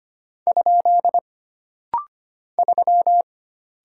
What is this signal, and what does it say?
Morse code, heard as a keyed tone, about 25 wpm, 700 Hz. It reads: ? 3
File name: Single Character (Letter:Number:Punctuation:Prosign) - Rapid-Fire - 25wpm